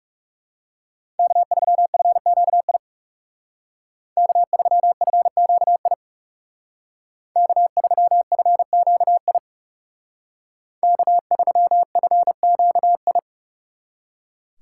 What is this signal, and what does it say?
Morse code, heard as a keyed tone, about 39 wpm, 700 Hz. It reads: K3FQI K3FQI K3FQI K3FQI